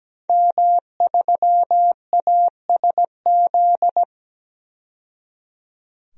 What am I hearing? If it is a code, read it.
Morse code, heard as a keyed tone, about 17 wpm, 700 Hz. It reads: M3ASZ
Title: DXCC-100 Call Signs - 5 Characters - 17wpm